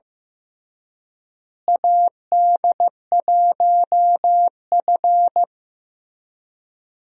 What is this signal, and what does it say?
Morse code, heard as a keyed tone, about 16 wpm, 700 Hz. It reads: AD1F
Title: USA Call Signs - 4 Characters - Repeated 2x - 15wpm